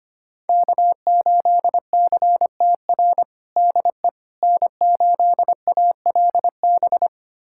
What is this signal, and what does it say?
Morse code, heard as a keyed tone, about 25 wpm, 700 Hz. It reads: K8CTR DE N8ALB
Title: USA Callsign DE Callsign - 5 Characters - 25wpm